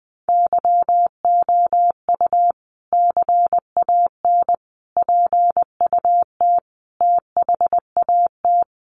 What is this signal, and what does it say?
Morse code, heard as a keyed tone, about 20 wpm, 700 Hz. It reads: YOU CAN PUT THAT
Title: Sentences from Top 1500 Words 20wpm